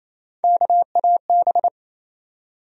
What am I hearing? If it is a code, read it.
Morse code, heard as a keyed tone, about 28 wpm, 700 Hz. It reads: KAB
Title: USA Call Sign Postfixes - 3 Letters 28wpm